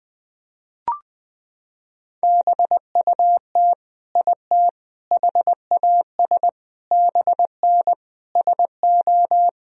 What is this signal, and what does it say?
Morse code, heard as a keyed tone, about 21 wpm, 700 Hz. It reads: BUT IT HAS BN SO
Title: Sentences from Top 100 Words and Abbreviations 20wpm